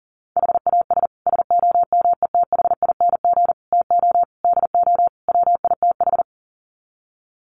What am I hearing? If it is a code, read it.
Morse code, heard as a keyed tone, about 40 wpm, 700 Hz. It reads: HAS SOMETHING TO DO WITH
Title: Sets of 5 Words Encore 40wpm